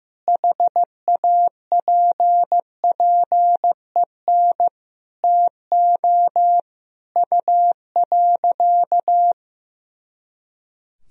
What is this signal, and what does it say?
Morse code, heard as a keyed tone, about 16 wpm, 700 Hz. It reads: HAPPEN TO U.
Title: Sentences from Top 500 Words and Abbreviations 15wpm